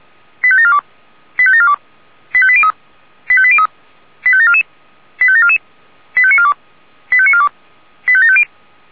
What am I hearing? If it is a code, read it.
Five-tone selective call (ZVEI), heard as quick runs of five tones, tones of about 70 ms. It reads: 85742 85742 85803 85803 85744 85744 85942 85942 85750